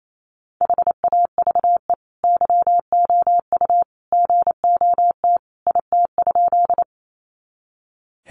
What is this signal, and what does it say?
Morse code, heard as a keyed tone, about 28 wpm, 700 Hz. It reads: HAVE YOU GOT IT?
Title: Sentences from Top 300 Words 28wpm